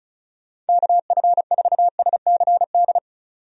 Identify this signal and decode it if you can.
Morse code, heard as a keyed tone, about 34 wpm, 700 Hz. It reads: KF4SCD
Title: USA Call Signs - 6 Characters - 35wpm